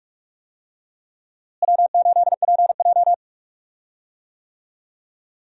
Morse code, heard as a keyed tone, about 46 wpm, 700 Hz. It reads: W8PJ
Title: USA Call Signs - 4 Characters - Repeated 3x - 45wpm